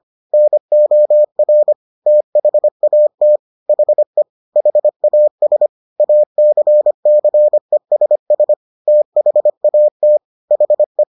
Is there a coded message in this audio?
Yes: Morse code from 0.3 to 11.0 s, heard as a keyed tone, about 25 wpm, 600 Hz. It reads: NOR THAT HE HAS ACCESS THAT HE